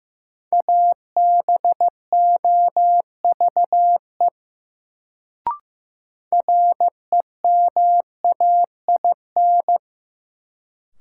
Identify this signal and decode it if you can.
Morse code, heard as a keyed tone, about 15 wpm, 700 Hz. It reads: ABOVE REMAIN
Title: Sentences from Top 1200 Words - New Word Drill Alt - 15wpm